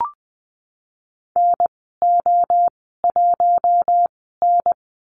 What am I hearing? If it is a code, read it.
Morse code, heard as a keyed tone, about 20 wpm, spaced as for 15 wpm, 700 Hz. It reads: NO1N
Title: 055 Lesson 10g - Letter r - Call Signs 20wpm